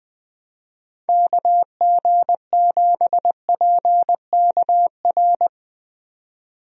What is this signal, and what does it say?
Morse code, heard as a keyed tone, about 20 wpm, 700 Hz. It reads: KG7PKR
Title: USA Call Signs - 6 Characters - 20wpm